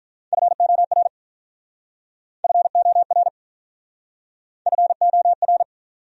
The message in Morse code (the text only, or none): FOR FOR FOR